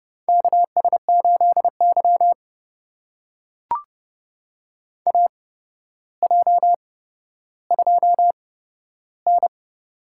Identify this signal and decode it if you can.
Morse code, heard as a keyed tone, about 31 wpm, spaced as for 13 wpm, 700 Hz. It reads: KS8Y AJ2N
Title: USA Call Signs - 4 Characters - 8x Character Spacing - 30wpm